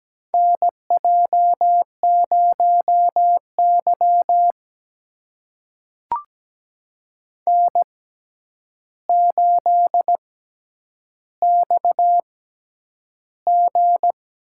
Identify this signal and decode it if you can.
Morse code, heard as a keyed tone, about 17 wpm, spaced as for 9 wpm, 700 Hz. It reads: NJ0Y N8XG
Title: USA Call Signs - 4 Characters - 6x Character Spacing - 17wpm